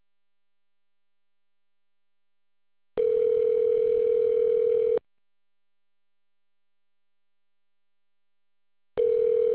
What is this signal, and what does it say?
Ringback tone (North America), 440 + 480 Hz.